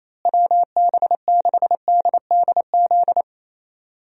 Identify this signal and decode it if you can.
Morse code, heard as a keyed tone, about 28 wpm, 700 Hz. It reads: WB6DDZ
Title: USA Call Signs - 6 Characters - Repeated 3x - 28wpm